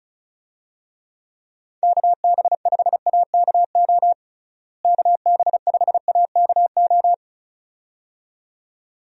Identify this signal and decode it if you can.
Morse code, heard as a keyed tone, about 34 wpm, 700 Hz. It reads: KB5AKO KB5AKO
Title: USA Call Signs - 6 Characters - Repeated 2x - 35wpm